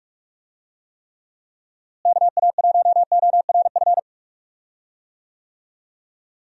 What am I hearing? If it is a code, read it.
Morse code, heard as a keyed tone, about 43 wpm, 700 Hz. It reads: KA1ORF
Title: USA Call Signs - 6 Characters - Repeated 3x - 45wpm